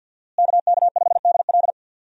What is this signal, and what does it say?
Morse code, heard as a keyed tone, about 51 wpm, 700 Hz. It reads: KK5DL
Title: USA Call Signs - 5 Characters - Repeated 3x - 50wpm